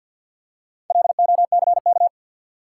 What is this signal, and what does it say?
Morse code, heard as a keyed tone, about 53 wpm, 700 Hz. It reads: LOCK
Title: Four Letter Words - 50wpm